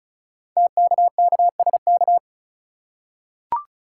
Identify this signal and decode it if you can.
Morse code, heard as a keyed tone, about 36 wpm, 690 Hz. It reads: TKKSK
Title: Code Groups - 5 Letters - 35wpm